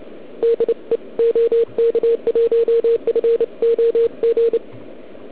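Morse code, heard as a keyed tone, about 29 wpm, 440 Hz. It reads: DE OK1FOG
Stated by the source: Morse from the OK QRP 2013 contest, run at 1 watt on a 12.3V lithium battery, Liberci, with OK1MKX also at 1W.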